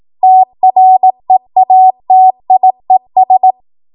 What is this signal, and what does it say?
Morse code, heard as a keyed tone, about 18 wpm, 750 Hz. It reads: TREATIES
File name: Morse / 18WPM